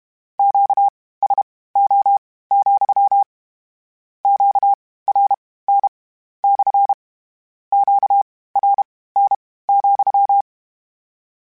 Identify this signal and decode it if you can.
Morse code, heard as a keyed tone, about 32 wpm, spaced as for 18 wpm, 800 Hz. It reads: QSO, QRN / QRN,
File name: Q Calls / Q Calls-mp3s / Q Calls SPSL2 @30